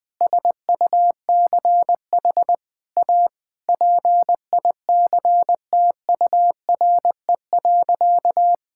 Morse code, heard as a keyed tone, about 21 wpm, 700 Hz. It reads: SUCH A PICTURE.
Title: Sentences from Top 1000 Words - Review Part 6 - 20wpm